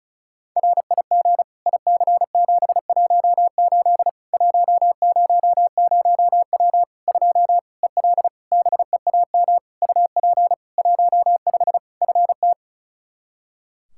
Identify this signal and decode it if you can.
Morse code, heard as a keyed tone, about 34 wpm, 700 Hz. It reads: RIG IC718 100W 2 EL BEAM UP 15 FT
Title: Practice QSO - Exchange 2 - Info - 35wpm